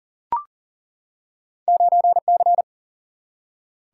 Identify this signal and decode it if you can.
Morse code, heard as a keyed tone, about 40 wpm, 700 Hz. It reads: 9C
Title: Single-Letter Single-Number 40wpm